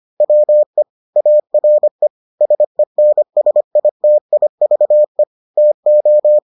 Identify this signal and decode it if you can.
Morse code, heard as a keyed tone, about 25 wpm, 600 Hz. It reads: WE ARE SENSITIVE TO